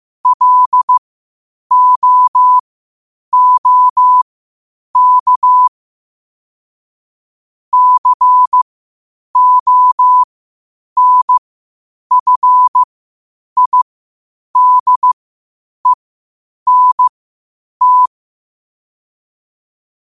Morse code, heard as a keyed tone, about 15 wpm, spaced as for 8 wpm, 1000 Hz. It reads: LOOK CONFIDENT
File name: qotdfiles / 8wpm